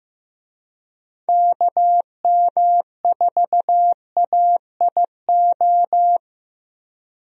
Morse code, heard as a keyed tone, about 15 wpm, 700 Hz. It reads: KM4AIO
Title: USA Call Signs - 6 Characters - 15wpm